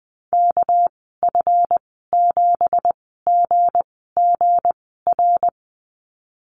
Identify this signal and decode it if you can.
Morse code, heard as a keyed tone, about 20 wpm, spaced as for 15 wpm, 700 Hz. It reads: KF7GGR
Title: 213 Lesson 34g - Number 8 - Call Signs 20wpm